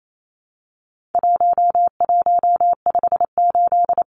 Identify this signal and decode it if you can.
Morse code, heard as a keyed tone, about 28 wpm, 700 Hz. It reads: 1158
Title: Numbers - 4-Digit 28wpm